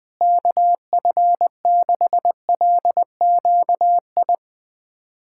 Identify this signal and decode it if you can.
Morse code, heard as a keyed tone, about 20 wpm, 700 Hz. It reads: KF6LQI
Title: USA Callsign DE Callsign - 6 Characters - 20wpm